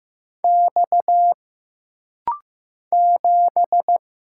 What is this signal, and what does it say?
Morse code, heard as a keyed tone, about 15 wpm, 700 Hz. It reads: X 7
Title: Single Letter-Number Alt - Rapid-Fire 15wpm